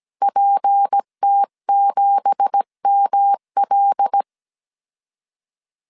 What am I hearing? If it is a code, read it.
Morse code, heard as a keyed tone, about 17 wpm, 800 Hz. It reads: PT7ML
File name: 1500C S B 16wpm